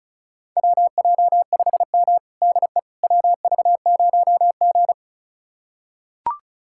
Morse code, heard as a keyed tone, about 36 wpm, 700 Hz. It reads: WJ5M DE WV0G
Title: USA Callsign DE Callsign - 4 Characters - 35wpm